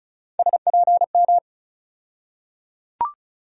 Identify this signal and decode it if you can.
Morse code, heard as a keyed tone, about 36 wpm, 700 Hz. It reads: SPM
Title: USA Call Sign Postfixes - 3 Letters 35wpm